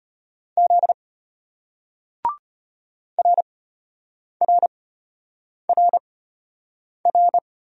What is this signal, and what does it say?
Morse code, heard as a keyed tone, about 32 wpm, spaced as for 8 wpm, 700 Hz. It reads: Z RRRR